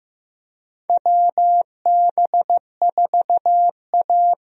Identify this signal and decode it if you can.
Morse code, heard as a keyed tone, about 15 wpm, 700 Hz. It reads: WB4A
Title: USA Call Signs - 4 Characters - Repeated 3x - 15wpm